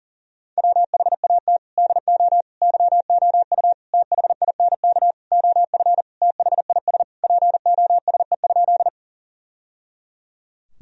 Morse code, heard as a keyed tone, about 37 wpm, 700 Hz. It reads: WHAT DO YOU THINK OF THIS POSE?
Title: Sentences from Top 1000 Words - Review Part 2 - 40wpm